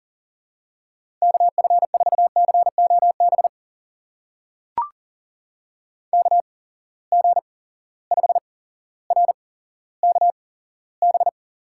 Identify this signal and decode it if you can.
Morse code, heard as a keyed tone, about 40 wpm, spaced as for 17 wpm, 700 Hz. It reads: KF4COB KG5RKB
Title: USA Call Signs - 6 Characters - 8x Character Spacing - 40wpm